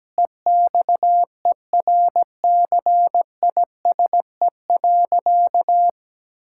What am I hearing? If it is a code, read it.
Morse code, heard as a keyed tone, about 17 wpm, 700 Hz. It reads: EXERCISE.
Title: Sentences from Top 1000 Words - Review Part 6 - 17wpm